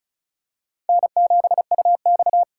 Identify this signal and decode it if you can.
Morse code, heard as a keyed tone, about 36 wpm, 700 Hz. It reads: N7UX
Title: USA Call Signs - 2x Character Spacing - 35wpm